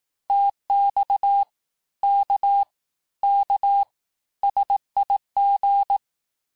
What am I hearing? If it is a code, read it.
Morse code, heard as a keyed tone, about 18 wpm, 800 Hz. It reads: TX K K SIG
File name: cours-du-lundi files / cours-1588-texte-abreviations-vitesse-17mpm-puis-22mpm